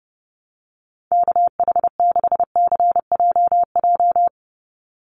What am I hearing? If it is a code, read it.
Morse code, heard as a keyed tone, about 30 wpm, 700 Hz. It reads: KH6CJJ
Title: Call Signs Repeated 2x - 30wpm